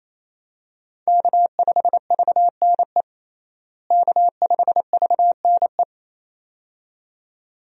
Morse code, heard as a keyed tone, about 28 wpm, 700 Hz. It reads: K5VNE K5VNE
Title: USA Call Signs - 5 Characters - Repeated 2x - 28wpm